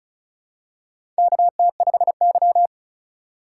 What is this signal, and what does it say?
Morse code, heard as a keyed tone, about 34 wpm, 700 Hz. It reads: KT5Y
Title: USA Call Signs - 8x Character Spacing - 35wpm